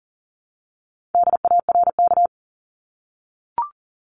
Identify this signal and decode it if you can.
Morse code, heard as a keyed tone, about 40 wpm, 700 Hz. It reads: DARK
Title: Top 500 Words 40wpm